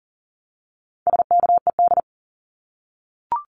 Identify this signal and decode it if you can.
Morse code, heard as a keyed tone, about 39 wpm, 700 Hz. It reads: SKED
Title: Abbreviations Repeated 3x - 40wpm